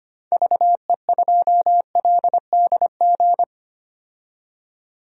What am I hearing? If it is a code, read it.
Morse code, heard as a keyed tone, about 25 wpm, 700 Hz. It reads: VE2LDG
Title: DXCC-100 Call Signs - 6 Characters - 25wpm